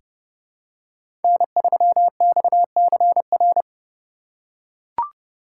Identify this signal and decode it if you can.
Morse code, heard as a keyed tone, about 31 wpm, 700 Hz. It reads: N3XCR
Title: USA Call Signs - 3x Character Spacing - 30wpm